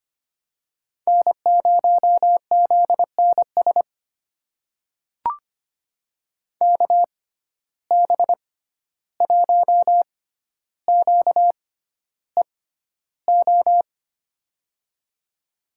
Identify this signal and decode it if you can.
Morse code, heard as a keyed tone, about 25 wpm, spaced as for 12 wpm, 700 Hz. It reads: N0ZNS KB1QEO
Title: USA Call Signs - 6x Character Spacing - 25wpm